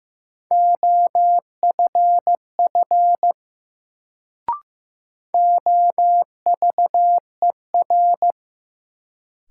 Morse code, heard as a keyed tone, about 15 wpm, 700 Hz. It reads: OFF OVER
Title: Sets of 2 Words - New Word Drill - 15wpm